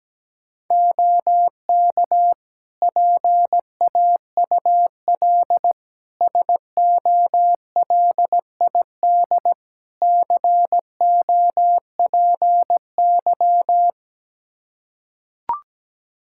Morse code, heard as a keyed tone, about 17 wpm, 700 Hz. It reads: OK PAUL SOLID COPY
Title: K1USN SST QSO Practice 17wpm